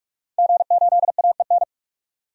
Morse code, heard as a keyed tone, about 45 wpm, 700 Hz. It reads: G8AEN
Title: DXCC-100 Call Signs - 5 Characters - 45wpm